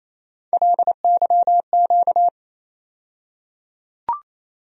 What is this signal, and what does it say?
Morse code, heard as a keyed tone, about 28 wpm, 700 Hz. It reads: LYQ
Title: USA Call Sign Postfixes - 3 Letters 28wpm